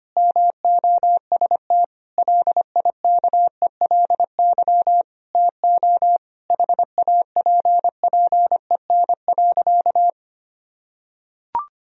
Morse code, heard as a keyed tone, about 25 wpm, 700 Hz. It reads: MOST LIKELY TO HAPPEN.
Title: Sentences from Top 1000 Words - Review Part 4 - 25wpm